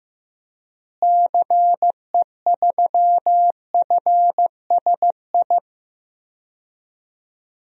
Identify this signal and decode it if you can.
Morse code, heard as a keyed tone, about 15 wpm, 700 Hz. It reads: CE3FSI
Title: DXCC-100 Call Signs - 6 Characters - Repeated 2x - 15wpm